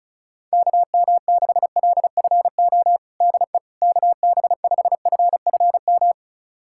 Morse code, heard as a keyed tone, about 36 wpm, 700 Hz. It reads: KM6LFO DE KB5FFM
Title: USA Callsign DE Callsign - 6 Characters - 35wpm